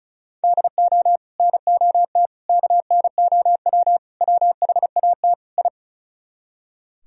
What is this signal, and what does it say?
Morse code, heard as a keyed tone, about 46 wpm, 700 Hz. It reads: DO NOT KNOW WHAT I